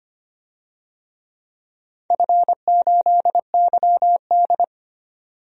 Morse code, heard as a keyed tone, about 25 wpm, 700 Hz. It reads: F8YD
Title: DXCC-100 Call Signs - 4 Characters - 25wpm